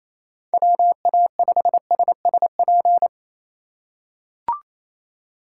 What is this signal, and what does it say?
Morse code, heard as a keyed tone, about 28 wpm, 700 Hz. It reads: WA5SSP
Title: USA Call Signs - 6 Characters - 4x Character Spacing - 28wpm